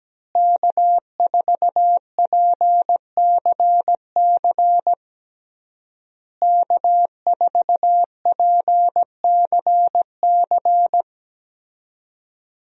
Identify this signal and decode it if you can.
Morse code, heard as a keyed tone, about 17 wpm, 700 Hz. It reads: K4PCC K4PCC
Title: USA Call Signs - 5 Characters - Repeated 3x - 17wpm